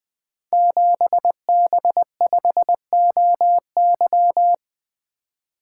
Morse code, heard as a keyed tone, about 20 wpm, 700 Hz. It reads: 7B5OY
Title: Code Groups - 5 Letter:Numbers - 20wpm